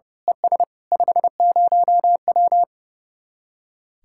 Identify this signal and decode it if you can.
Morse code, heard as a keyed tone, about 28 wpm, 700 Hz. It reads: ES 50W
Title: Speed-Racing Common QSO Phrases Encore - 30wpm